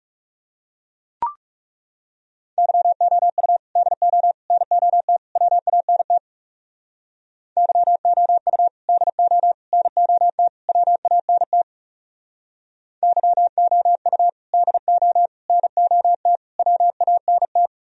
Morse code, heard as a keyed tone, about 37 wpm, 700 Hz. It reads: YOU DO NOT WANT YOU DO NOT WANT YOU DO NOT WANT
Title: Speed-Racing - Sets of 4 Words 30wpm